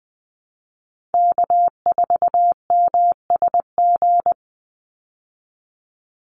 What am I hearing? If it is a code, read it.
Morse code, heard as a keyed tone, about 20 wpm, 700 Hz. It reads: K4MSG
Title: Call Signs Repeated 2x - 20wpm